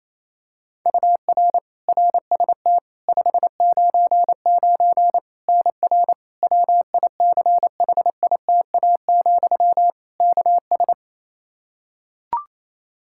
Morse code, heard as a keyed tone, about 28 wpm, 700 Hz. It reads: UR RST 599 NR WICHITA, KS
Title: K1USN SST QSO Practice 28wpm